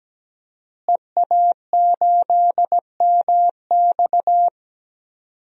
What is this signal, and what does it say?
Morse code, heard as a keyed tone, about 17 wpm, 700 Hz. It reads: EA8MX
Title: DXCC-100 Call Signs - 5 Characters - Repeated 3x - 17wpm